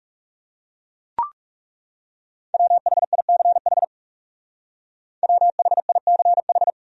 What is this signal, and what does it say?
Morse code, heard as a keyed tone, about 36 wpm, 700 Hz. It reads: WHICH WHICH
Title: Speed-Racing Top 100 Words 30wpm